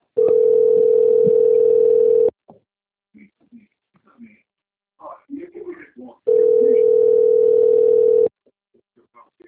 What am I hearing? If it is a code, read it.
Ringback tone (North America), 440 + 480 Hz, with speech.
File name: out- - - - - .86006